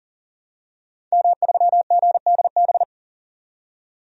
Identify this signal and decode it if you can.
Morse code, heard as a keyed tone, about 40 wpm, 700 Hz. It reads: M3GDB
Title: DXCC-100 Call Signs - 5 Characters - 40wpm